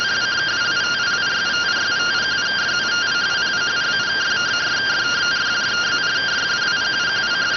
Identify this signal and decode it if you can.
Two-tone teleprinter signal, about 170 Hz shift.